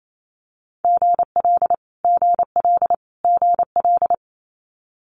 Morse code, heard as a keyed tone, about 28 wpm, 700 Hz. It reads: GL GL GL
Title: Multi-letter Phonemes Repeated 3x 28wpm